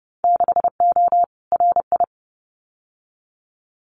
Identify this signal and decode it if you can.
Morse code, heard as a keyed tone, about 30 wpm, 700 Hz. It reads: 6O RI
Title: Winter Field Day Exchanges Encore 30wpm